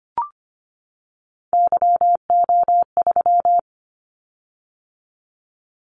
Morse code, heard as a keyed tone, about 25 wpm, 700 Hz. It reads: YO3
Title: Callsign Prefixes for Top 25 DXCC - 25wpm